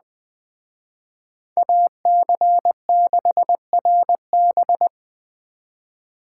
Morse code, heard as a keyed tone, about 20 wpm, 700 Hz. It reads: AC6RB
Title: Speed Racing - USA 2x2 Call Signs 17wpm